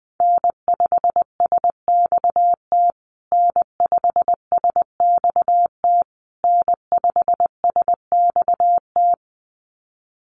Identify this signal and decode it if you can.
Morse code, heard as a keyed tone, about 20 wpm, 700 Hz. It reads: N5SXT N5SXT N5SXT
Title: USA Call Signs Repeated 3x 20wpm